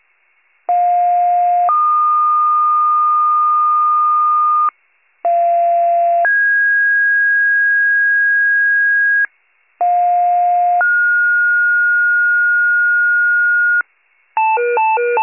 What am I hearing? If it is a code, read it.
Two-tone page, heard as a one-second tone, then a three-second tone, about 710 then 1220 Hz, about 690 then 1650 Hz, about 710 then 1430 Hz, with noise. It is heard from 0.7 to 13.8 s.